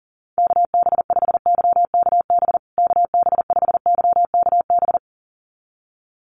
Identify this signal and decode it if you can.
Morse code, heard as a keyed tone, about 40 wpm, 700 Hz. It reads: KB5YKB KB5YKB
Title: USA Call Signs Repeated 2x 40wpm